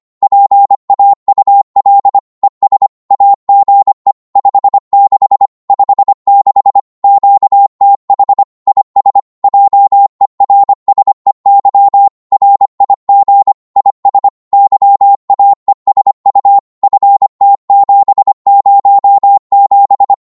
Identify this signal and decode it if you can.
Morse code, heard as a keyed tone, about 25 wpm, 800 Hz. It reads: PAUL ES AGE 56 56 QTH IS JERSEY RIG IS YAESU FT707
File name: Morse / qso-25wpm